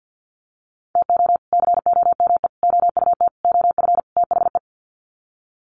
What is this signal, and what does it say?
Morse code, heard as a keyed tone, about 50 wpm, 700 Hz. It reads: TO COME OUT OF THE
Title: Sets of 5 Words Encore2 50wpm